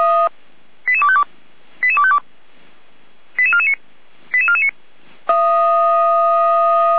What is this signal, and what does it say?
Five-tone selective call (ZVEI), heard as quick runs of five tones, tones of about 70 ms. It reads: 88252 88252 88449 88449